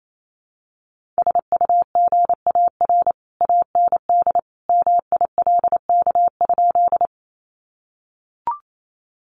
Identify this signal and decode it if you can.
Morse code, heard as a keyed tone, about 28 wpm, 700 Hz. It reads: SUGAR AND MILK?
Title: Sentences from Top 1500 Words 28wpm